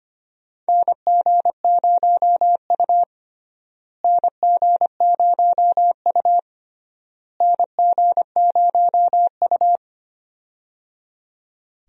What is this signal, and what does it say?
Morse code, heard as a keyed tone, about 25 wpm, 700 Hz. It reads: NG0U NG0U NG0U
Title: USA Call Signs - 4 Characters - Repeated 3x - 25wpm